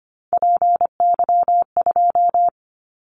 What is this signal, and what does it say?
Morse code, heard as a keyed tone, about 25 wpm, 700 Hz. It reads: PY2
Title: Callsign Prefixes for Top 50 DXCC - 25wpm